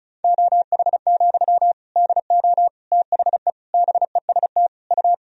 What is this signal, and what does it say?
Morse code, heard as a keyed tone, about 36 wpm, 700 Hz. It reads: OH, DO THE BEST U
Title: Sentences from Top 500 Words and Abbreviations 35wpm